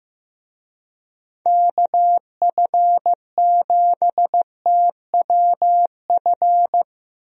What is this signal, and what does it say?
Morse code, heard as a keyed tone, about 16 wpm, 700 Hz. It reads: KF7TWF
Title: USA Call Signs - 6 Characters - Repeated 3x - 15wpm